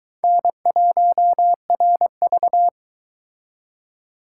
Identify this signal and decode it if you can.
Morse code, heard as a keyed tone, about 23 wpm, 700 Hz. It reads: N1RV